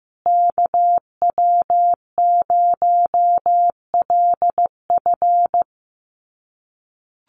Morse code, heard as a keyed tone, about 15 wpm, 700 Hz. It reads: KW0LF
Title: USA Call Signs Repeated 3x 15wpm